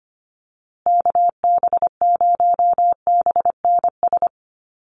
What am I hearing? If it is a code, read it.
Morse code, heard as a keyed tone, about 25 wpm, 700 Hz. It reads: KB0BNS
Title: USA Call Signs Repeated 2x 25wpm